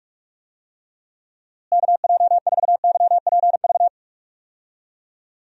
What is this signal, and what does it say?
Morse code, heard as a keyed tone, about 46 wpm, 700 Hz. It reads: KJ4YPV
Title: USA Call Signs - 6 Characters - Repeated 3x - 45wpm